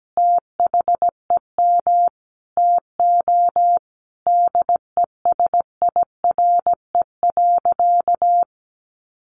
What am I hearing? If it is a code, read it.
Morse code, heard as a keyed tone, about 17 wpm, 700 Hz. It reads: THEM TO DESIRE.
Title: Sentences from Top 1100 Words 17wpm